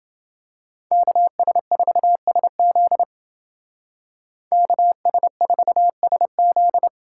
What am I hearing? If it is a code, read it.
Morse code, heard as a keyed tone, about 29 wpm, 700 Hz. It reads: KS4SZ KS4SZ